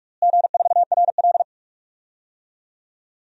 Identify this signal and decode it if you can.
Morse code, heard as a keyed tone, about 42 wpm, 700 Hz. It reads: G4RL
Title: DXCC-100 Call Signs - 4 Characters - Repeated 3x - 45wpm